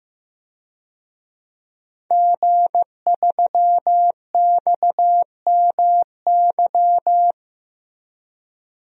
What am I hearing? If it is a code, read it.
Morse code, heard as a keyed tone, about 15 wpm, 700 Hz. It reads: G3XMY
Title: DXCC-100 Call Signs - 5 Characters - Repeated 2x - 15wpm